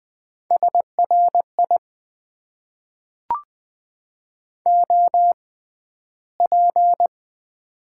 Morse code, heard as a keyed tone, about 20 wpm, spaced as for 12 wpm, 700 Hz. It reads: SRI OP
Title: Top 90 QSO Elements - 6x Character Spacing - 20wpm